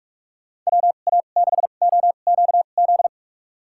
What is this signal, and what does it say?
Morse code, heard as a keyed tone, about 41 wpm, spaced as for 31 wpm, 700 Hz. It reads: WA6OQZ
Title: USA Call Signs - 6 Characters - 2x Character Spacing - 45wpm